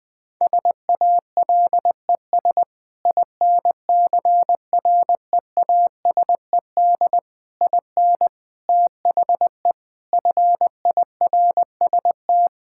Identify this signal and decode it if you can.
Morse code, heard as a keyed tone, about 20 wpm, 700 Hz. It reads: SALES INCREASED IN THE FIRST
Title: Sentences with 8-Word Phrases - Spoken Beginning - 20wpm